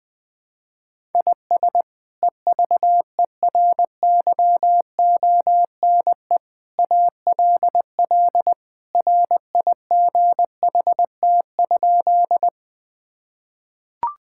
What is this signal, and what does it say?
Morse code, heard as a keyed tone, about 20 wpm, 700 Hz. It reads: IS EVERYONE ALL RIGHT?
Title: Sentences from Top 1000 Words - Review Part 5 - 20wpm